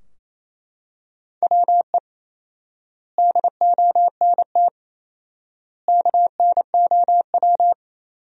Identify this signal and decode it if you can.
Morse code, heard as a keyed tone, about 28 wpm, 700 Hz. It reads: WE DONT KNOW
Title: 4x Word-Spacing - Sentences from Top 400 Words - 28wpm